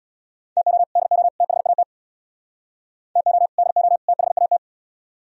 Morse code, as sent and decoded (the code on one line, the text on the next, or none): - --- / -.. --- / - .... .- - / - --- / -.. --- / - .... .- -
TO DO THAT TO DO THAT